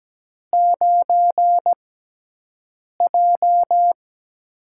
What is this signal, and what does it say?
Morse code, heard as a keyed tone, about 17 wpm, spaced as for 6 wpm, 700 Hz. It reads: 9J
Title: USA Call Signs - 6x Character Spacing - 17wpm